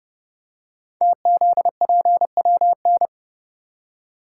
Morse code, heard as a keyed tone, about 30 wpm, 700 Hz. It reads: TZPWN
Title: Code Groups - 5 Letter:Numbers - 30wpm